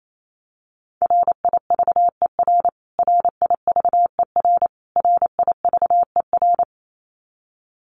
Morse code, heard as a keyed tone, about 28 wpm, 700 Hz. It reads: RIVER RIVER RIVER
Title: Top 500 Words Repeated 28wpm